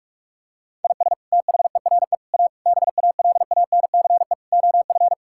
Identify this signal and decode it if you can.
Morse code, heard as a keyed tone, about 46 wpm, 700 Hz. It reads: IS THERE A BALANCE OF
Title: Sentences from Top 1000 Words - Review Part 6 - 45wpm